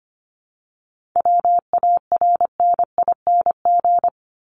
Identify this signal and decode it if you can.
Morse code, heard as a keyed tone, about 25 wpm, 700 Hz. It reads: WARNING
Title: Top 1751-2000 Words 25wpm